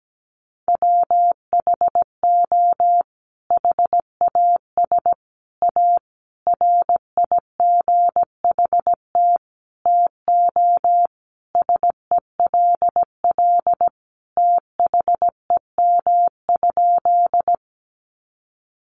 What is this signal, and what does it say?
Morse code, heard as a keyed tone, about 17 wpm, 700 Hz. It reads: WHO HAS A RIGHT TO SELL THEM?
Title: Sentences from Top 1100 Words 17wpm